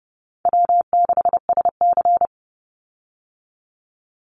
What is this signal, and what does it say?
Morse code, heard as a keyed tone, about 30 wpm, 700 Hz. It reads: W6SC
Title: Call Signs Repeated 2x - 30wpm